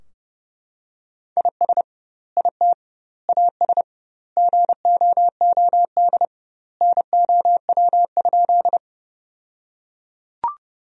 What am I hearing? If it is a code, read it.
Morse code, heard as a keyed tone, about 30 wpm, 700 Hz. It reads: IS IT AS GOOD NOW?
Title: 2x Word-Spacing - Sentences from Top 200 Words - 30wpm